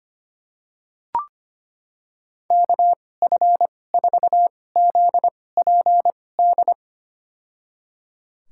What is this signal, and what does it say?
Morse code, heard as a keyed tone, about 25 wpm, spaced as for 18 wpm, 700 Hz. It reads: KF4ZPD
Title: USA Call Signs - 6 Characters - 2x Character Spacing - 25wpm